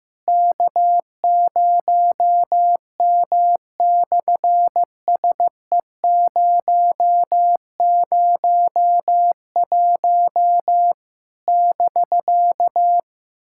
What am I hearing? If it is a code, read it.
Morse code, heard as a keyed tone, about 15 wpm, 700 Hz. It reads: K0M/SE001 <BK>
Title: SOTA QSO Practice 15wpm